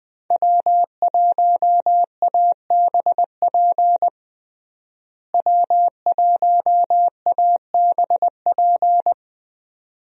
Morse code, heard as a keyed tone, about 20 wpm, 700 Hz. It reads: W1ABP W1ABP
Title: USA Call Signs - 5 Characters - Repeated 2x - 20wpm